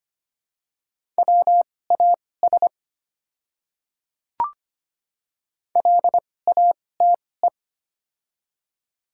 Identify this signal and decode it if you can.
Morse code, heard as a keyed tone, about 25 wpm, spaced as for 18 wpm, 690 Hz. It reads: WAS LATE
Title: 2x Character Spacing - Top 300 Words - 25wpm